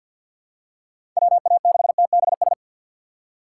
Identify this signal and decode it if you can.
Morse code, heard as a keyed tone, about 52 wpm, 700 Hz. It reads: WA6TBS
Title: USA Call Signs - 6 Characters - 4x Character Spacing - 50wpm